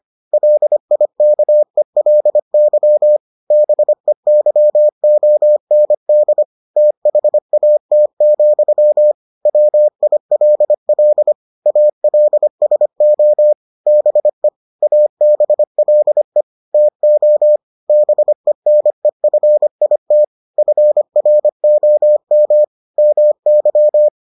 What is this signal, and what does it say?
Morse code, heard as a keyed tone, about 25 wpm, 600 Hz. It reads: LIKELY BEYOND THAT, WILL ALSO BE ABLE TO BENEFIT FROM MY